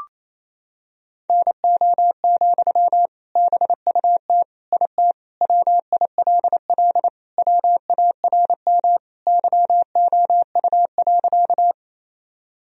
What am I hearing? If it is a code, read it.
Morse code, heard as a keyed tone, about 28 wpm, 700 Hz. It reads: NO, BUT IT WILL WARM YOU.
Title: Sentences from Top 1000 Words - Review Part 2 - 28wpm